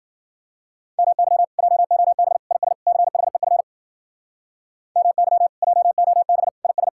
Morse code, heard as a keyed tone, about 50 wpm, 700 Hz. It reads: MY JOB IS CHEF MY JOB IS